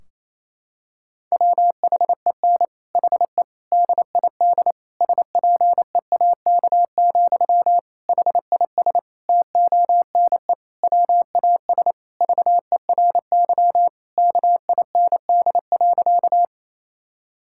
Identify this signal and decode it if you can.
Morse code, heard as a keyed tone, about 27 wpm, 700 Hz. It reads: WHEN HE DID SPEAK, HIS TONE WAS VERY KIND.
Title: Sentences from Top 1600 Words 28wpm